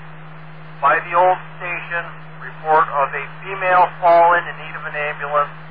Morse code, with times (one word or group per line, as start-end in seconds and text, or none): none